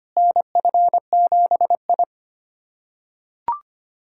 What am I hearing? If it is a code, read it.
Morse code, heard as a keyed tone, about 25 wpm, 700 Hz. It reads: NF7I
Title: USA Callsign DE Callsign - 4 Characters - 25wpm